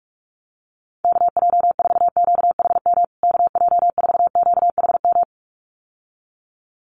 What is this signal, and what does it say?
Morse code, heard as a keyed tone, about 44 wpm, 700 Hz. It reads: KJ4QHM KJ4QHM
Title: USA Call Signs Repeated 2x 45wpm